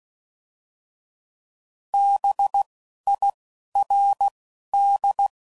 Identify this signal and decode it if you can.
Morse code, heard as a keyed tone, about 16 wpm, spaced as for 11 wpm, 800 Hz. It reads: BIRD
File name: Suffixes / General / Houses